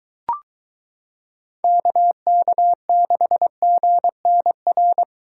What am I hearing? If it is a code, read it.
Morse code, heard as a keyed tone, about 23 wpm, 700 Hz. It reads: KK6GNR